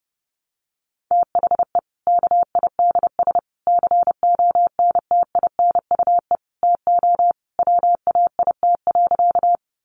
Morse code, heard as a keyed tone, about 30 wpm, 700 Hz. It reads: THE KIDS CONTINUE TO WAIT.
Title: Easy Phrases 30wpm